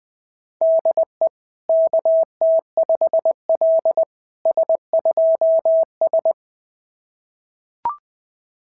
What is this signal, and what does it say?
Morse code, heard as a keyed tone, about 20 wpm, 690 Hz. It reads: DE KT5L S2S
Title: SOTA QSO Practice 20wpm